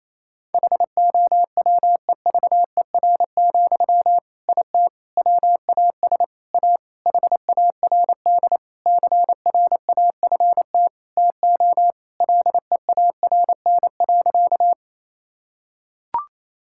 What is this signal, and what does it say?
Morse code, heard as a keyed tone, about 28 wpm, 700 Hz. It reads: HOWEVER, IT WAS A HARD CRAFT TO LEARN.
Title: Sentences from Top 1000 Words - Review Part 6 - 28wpm